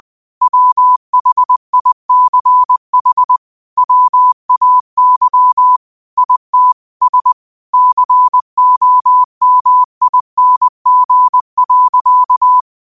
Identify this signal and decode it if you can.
Morse code, heard as a keyed tone, about 20 wpm, 1000 Hz. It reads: WHICH WAY IT S COMING.
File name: 20wpm